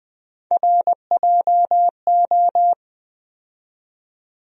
Morse code, heard as a keyed tone, about 20 wpm, 700 Hz. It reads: RJO
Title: USA Call Sign Postfixes - 3 Letters 20wpm